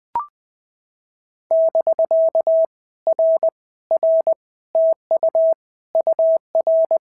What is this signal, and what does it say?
Morse code, heard as a keyed tone, about 19 wpm, 650 Hz. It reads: <BK> R R TU UR
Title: POTA QSO Practice - 20wpm